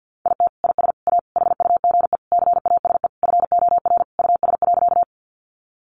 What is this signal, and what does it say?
Morse code, heard as a keyed tone, about 50 wpm, 700 Hz. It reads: IT IS A HUGE CASE FOR US.
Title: Sentences from Top 1100 Words 50wpm